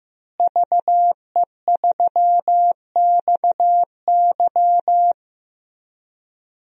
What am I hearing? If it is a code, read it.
Morse code, heard as a keyed tone, about 15 wpm, 700 Hz. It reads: VE3XY
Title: DXCC-100 Call Signs - 5 Characters - Repeated 3x - 15wpm